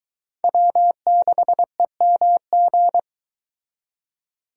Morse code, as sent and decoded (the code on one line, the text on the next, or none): .-- -.... -- --.
W6EMG